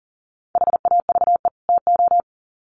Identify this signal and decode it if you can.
Morse code, heard as a keyed tone, about 38 wpm, 700 Hz. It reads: HAVE TO
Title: Sets of 2 Words 40wpm